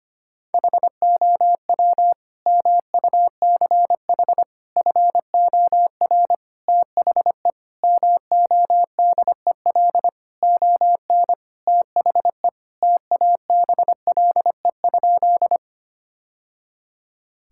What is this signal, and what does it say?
Morse code, heard as a keyed tone, about 25 wpm, 700 Hz. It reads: HOW MUCH FOR THE MODEL ON THE TABLE?
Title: Sentences from Top 1000 Words - Review Part 5 - 25wpm